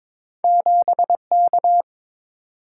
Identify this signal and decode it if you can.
Morse code, heard as a keyed tone, about 23 wpm, 700 Hz. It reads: 7K
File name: practice / 22wpm / Two Letter-Number Combinations - 22wpm